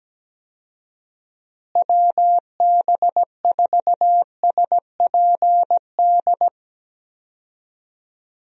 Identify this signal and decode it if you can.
Morse code, heard as a keyed tone, about 17 wpm, 700 Hz. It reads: WB4SPD
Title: USA Call Signs - 6 Characters - Repeated 2x - 17wpm